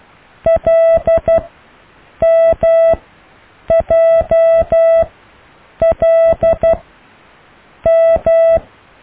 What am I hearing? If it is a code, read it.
Morse code, heard as a keyed tone, about 12 wpm, spaced as for 7 wpm, 660 Hz. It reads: LMJLM
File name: cw / IJLMU